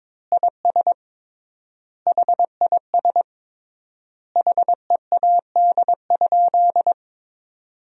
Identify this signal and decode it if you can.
Morse code, heard as a keyed tone, about 21 wpm, 700 Hz. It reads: IS HIS HEAD?
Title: 3x Word-Spacing - Sentences from Top 300 Words - 22wpm